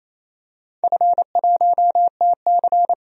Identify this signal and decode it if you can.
Morse code, heard as a keyed tone, about 28 wpm, 700 Hz. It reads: F1TC
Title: DXCC-100 Call Signs - 4 Characters - 28wpm